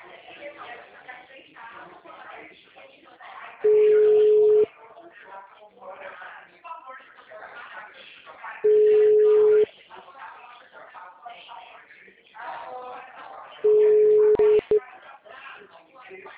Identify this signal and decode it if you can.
Ringback tone (Europe, 425 Hz), with speech.
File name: out- -2001- - - .37173